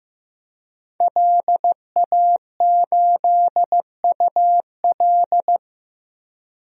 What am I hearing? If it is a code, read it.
Morse code, heard as a keyed tone, about 15 wpm, 700 Hz. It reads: LA8UL
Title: DXCC-100 Call Signs - 5 Characters - Repeated 2x - 15wpm